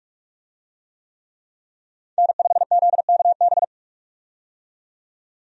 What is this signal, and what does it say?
Morse code, heard as a keyed tone, about 42 wpm, 700 Hz. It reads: N5ZKB
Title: USA Call Signs - 5 Characters - 45wpm